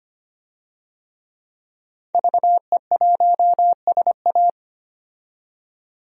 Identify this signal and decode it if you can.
Morse code, heard as a keyed tone, about 25 wpm, 700 Hz. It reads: VE1SA
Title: DXCC-100 Call Signs - 5 Characters - 25wpm